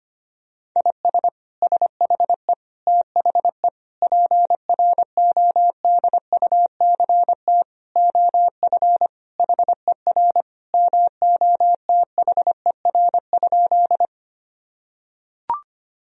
Morse code, heard as a keyed tone, about 25 wpm, 700 Hz. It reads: IS SHE THE PRODUCT OF HER MOTHER?
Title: Sentences from Top 1000 Words - Review Part 2 - 25wpm